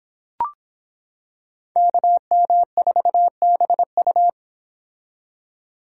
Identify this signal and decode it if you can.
Morse code, heard as a keyed tone, about 26 wpm, 700 Hz. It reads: KM4BU